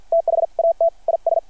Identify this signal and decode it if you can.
Morse code, heard as a keyed tone, about 53 wpm, spaced as for 44 wpm, 660 Hz. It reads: THAT IS
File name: archived / CRSnet